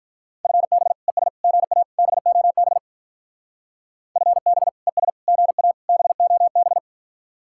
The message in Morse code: FB ES GA BOB FB ES GA BOB

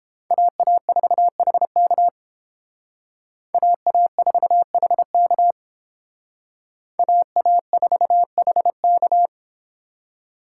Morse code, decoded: AA4HK AA4HK AA4HK

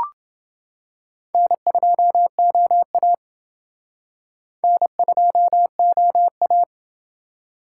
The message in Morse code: N2OA N2OA

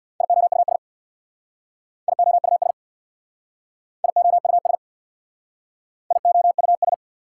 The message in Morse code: IOUS IOUS IOUS IOUS

